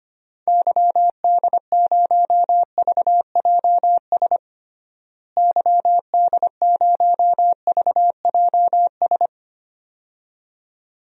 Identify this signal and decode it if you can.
Morse code, heard as a keyed tone, about 25 wpm, 700 Hz. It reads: YD0VJS YD0VJS
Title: DXCC-100 Call Signs - 6 Characters - Repeated 2x - 25wpm